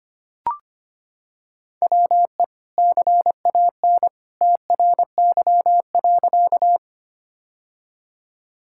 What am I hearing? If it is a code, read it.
Morse code, heard as a keyed tone, about 25 wpm, 700 Hz. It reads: WE CAN TRY.
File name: practice / 25wpm / Sentences with 2-Word Phrases - 25wpm